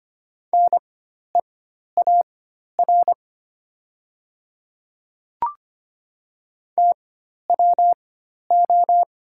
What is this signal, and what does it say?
Morse code, heard as a keyed tone, about 25 wpm, spaced as for 12 wpm, 690 Hz. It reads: NEAR TWO
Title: 4x Character Spacing - Top 200 Words - 25wpm